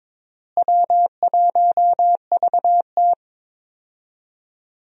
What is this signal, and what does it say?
Morse code, heard as a keyed tone, about 21 wpm, 700 Hz. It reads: W1VT
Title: USA Call Signs - 4 Characters - Repeated 3x - 22wpm